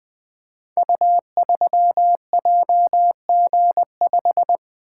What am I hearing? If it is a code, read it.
Morse code, heard as a keyed tone, about 20 wpm, 700 Hz. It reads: U3JG5
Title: Code Groups - 5 Letter:Numbers - 20wpm